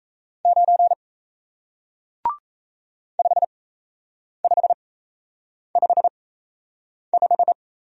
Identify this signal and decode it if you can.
Morse code, heard as a keyed tone, about 38 wpm, spaced as for 8 wpm, 700 Hz. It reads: 9 5555